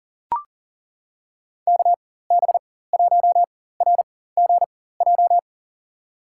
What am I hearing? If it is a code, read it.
Morse code, heard as a keyed tone, about 40 wpm, spaced as for 19 wpm, 700 Hz. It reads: KB1RGJ
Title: USA Call Signs - 6 Characters - 4x Character Spacing - 40wpm